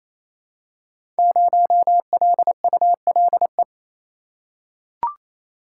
Morse code, heard as a keyed tone, about 29 wpm, 700 Hz. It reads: 0LULE
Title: Code Groups - 5 Letter:Numbers - 28wpm